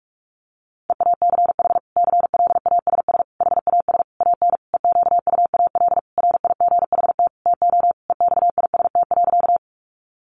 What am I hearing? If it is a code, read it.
Morse code, heard as a keyed tone, about 44 wpm, 700 Hz. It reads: EACH CLASS HAS AN EQUAL RIGHT TO EXIST.
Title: Sentences from Top 1400 Words 45wpm